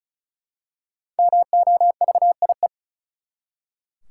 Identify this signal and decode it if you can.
Morse code, heard as a keyed tone, about 36 wpm, 700 Hz. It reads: MOVIE
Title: Five Letter Words - 35wpm